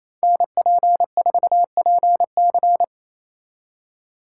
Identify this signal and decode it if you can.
Morse code, heard as a keyed tone, about 33 wpm, 700 Hz. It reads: NP4PC
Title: Speed Racing - USA 2x2 Call Signs 28wpm